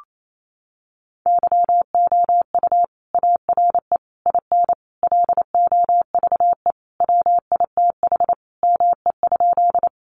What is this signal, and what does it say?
Morse code, heard as a keyed tone, about 28 wpm, 700 Hz. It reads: YOU ARE IN LOVE WITH ME?
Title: Sentences from Top 400 Words 28wpm